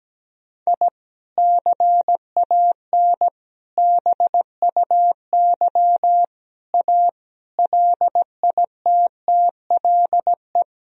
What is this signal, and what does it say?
Morse code, heard as a keyed tone, about 17 wpm, 700 Hz. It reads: I CAN BUY A LITTLE
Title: Sentences from Top 1000 Words - Review Part 1 - 17wpm